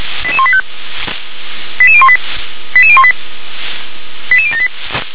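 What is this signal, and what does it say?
Five-tone selective call (ZVEI), heard as quick runs of five tones, tones of about 70 ms. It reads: 88185 88018 88018 88078
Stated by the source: ZVEI five-tone selective call received on SDR.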